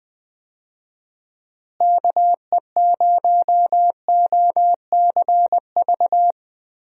Morse code, heard as a keyed tone, about 21 wpm, 700 Hz. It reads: KE0OCV
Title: USA Call Signs - 6 Characters - Repeated 2x - 20wpm